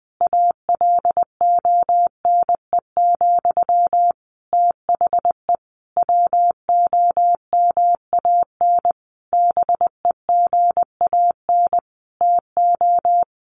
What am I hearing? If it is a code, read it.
Morse code, heard as a keyed tone, about 20 wpm, 700 Hz. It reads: ALONE, THE WOMAN BEGAN TO
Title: Sentences from Top 1100 Words 20wpm